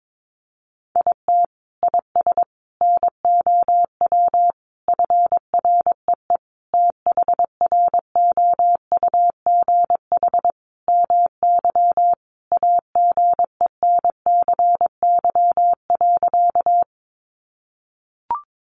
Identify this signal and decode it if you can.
Morse code, heard as a keyed tone, about 22 wpm, 700 Hz. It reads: IT IS NOW FREE THROUGH MY AGENCY.
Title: Sentences from Top 1300 Words 22wpm